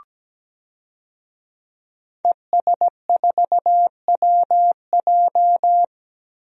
Morse code, heard as a keyed tone, about 17 wpm, 700 Hz. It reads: ES4WJ
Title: DXCC-100 Call Signs - 5 Characters - 17wpm